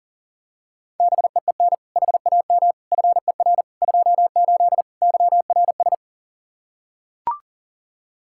Morse code, heard as a keyed tone, about 40 wpm, 700 Hz. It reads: BEEN HAM FER 28 YRS
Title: K1USN SST QSO Practice 40wpm